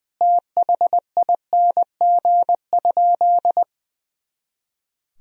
Morse code, heard as a keyed tone, about 21 wpm, 700 Hz. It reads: THING?
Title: Sentences from Top 200 Words and Abbreviations 20wpm